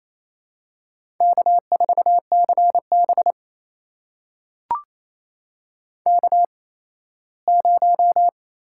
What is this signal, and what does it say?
Morse code, heard as a keyed tone, about 27 wpm, spaced as for 17 wpm, 700 Hz. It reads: K4CB K0
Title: USA Call Signs - 4 Characters - 8x Character Spacing - 28wpm